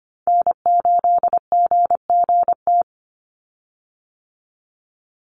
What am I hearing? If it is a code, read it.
Morse code, heard as a keyed tone, about 25 wpm, 700 Hz. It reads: N8GGT
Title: USA Call Signs 25wpm